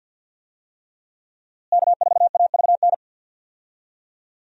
Morse code, heard as a keyed tone, about 51 wpm, 700 Hz. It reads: K4AVN
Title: USA Call Signs - 5 Characters - Repeated 3x - 50wpm